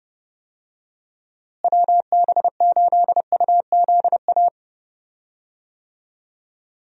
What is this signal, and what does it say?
Morse code, heard as a keyed tone, about 30 wpm, 700 Hz. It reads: WB8UZA
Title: USA Call Signs - 6 Characters - Repeated 2x - 30wpm